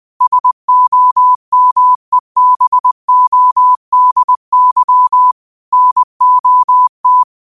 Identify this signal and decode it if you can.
Morse code, heard as a keyed tone, about 20 wpm, 1000 Hz. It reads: SOMEBODY NOT
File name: qotdfiles / 20wpm